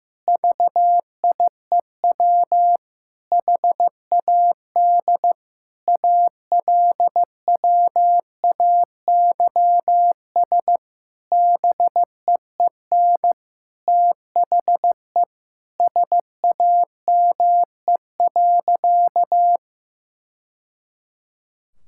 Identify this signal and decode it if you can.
Morse code, heard as a keyed tone, about 15 wpm, 700 Hz. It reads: VIEW HAD ALWAYS BEEN THE SAME.
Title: Sentences from Top 1000 Words - Review Part 6 - 15wpm